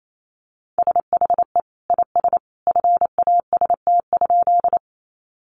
Morse code, heard as a keyed tone, about 28 wpm, 700 Hz. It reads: SHE IS FAST?
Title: Sentences from Top 400 Words 28wpm